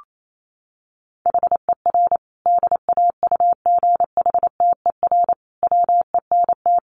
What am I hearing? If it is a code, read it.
Morse code, heard as a keyed tone, about 28 wpm, 700 Hz. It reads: HER DAUGHTER WENT
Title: Sentences from Top 1500 Words 28wpm